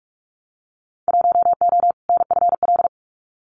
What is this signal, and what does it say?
Morse code, heard as a keyed tone, about 46 wpm, 700 Hz. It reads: 1O NFL
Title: Winter Field Day Exchanges 45wpm